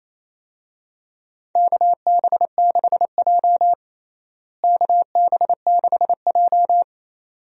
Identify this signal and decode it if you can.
Morse code, heard as a keyed tone, about 27 wpm, 700 Hz. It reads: KB6J KB6J
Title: USA Call Signs - 4 Characters - Repeated 2x - 28wpm